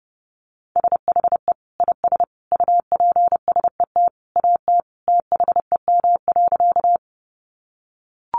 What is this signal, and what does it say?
Morse code, heard as a keyed tone, about 30 wpm, 700 Hz. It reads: SHE IS UPSET AT THEM.
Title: Sentences from Top 800 Words 30wpm